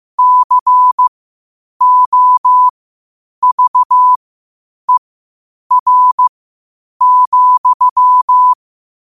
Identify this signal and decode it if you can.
Morse code, heard as a keyed tone, about 15 wpm, spaced as for 8 wpm, 1000 Hz. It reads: COVER,
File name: qotdfiles / 8wpm